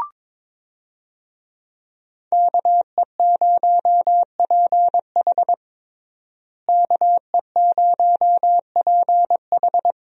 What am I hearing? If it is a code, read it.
Morse code, heard as a keyed tone, about 22 wpm, 700 Hz. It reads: KE0PH KE0PH
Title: USA Call Signs - 5 Characters - Repeated 3x - 22wpm